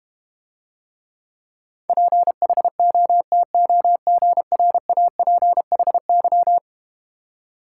Morse code, heard as a keyed tone, about 31 wpm, 700 Hz. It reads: PHOTOGRAPHY
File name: Easy Words - Speed-Racing with Hints - 25wpm